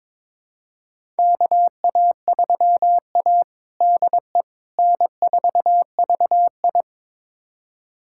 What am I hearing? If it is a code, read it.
Morse code, heard as a keyed tone, about 22 wpm, 700 Hz. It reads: KA3A DE N4VI
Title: USA Callsign DE Callsign - 4 Characters - 22wpm